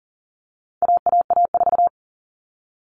Morse code, heard as a keyed tone, about 40 wpm, 700 Hz. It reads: AAA4
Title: All Cut Numbers - 4-Digit 40wpm